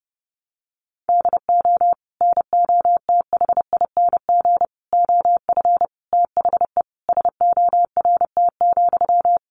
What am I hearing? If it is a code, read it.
Morse code, heard as a keyed tone, about 30 wpm, 700 Hz. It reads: DO NOTHING OF THE SORT,
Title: Sentences from Top 1300 Words 30wpm